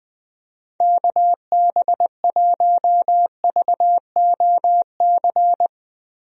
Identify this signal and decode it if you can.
Morse code, heard as a keyed tone, about 20 wpm, 700 Hz. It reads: KB1VOC
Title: USA Call Signs - 6 Characters - Repeated 2x - 20wpm